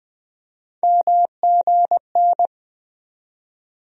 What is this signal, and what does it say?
Morse code, heard as a keyed tone, about 21 wpm, 700 Hz. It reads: MGN
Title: USA Call Sign Postfixes - 3 Letters 20wpm